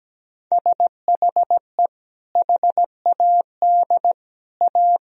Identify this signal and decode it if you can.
Morse code, heard as a keyed tone, about 17 wpm, 700 Hz. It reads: SHE HAD A
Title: Sentences from Top 1000 Words - Review Part 2 - 17wpm